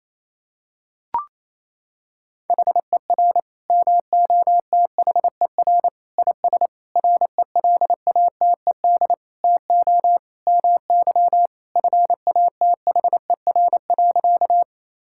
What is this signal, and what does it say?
Morse code, heard as a keyed tone, about 28 wpm, 700 Hz. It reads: HER MOTHER IS RELATED TO MY FATHER.
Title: Sentences from Top 1000 Words - Review Part 5 - 28wpm